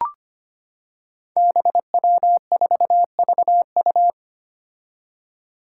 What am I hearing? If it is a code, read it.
Morse code, heard as a keyed tone, about 25 wpm, 700 Hz. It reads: BW4VU
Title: Five Challenging Letter-Number Combinations 25wpm